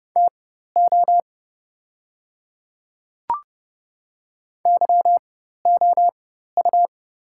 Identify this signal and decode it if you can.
Morse code, heard as a keyed tone, about 30 wpm, spaced as for 14 wpm, 690 Hz. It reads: TO YOU
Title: 4x Character Spacing - Top 500 Words - 30wpm